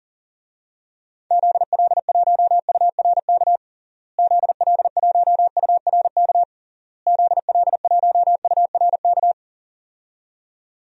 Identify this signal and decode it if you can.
Morse code, heard as a keyed tone, about 37 wpm, 700 Hz. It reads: ZL1URK ZL1URK ZL1URK
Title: DXCC-100 Call Signs - 6 Characters - Repeated 3x - 40wpm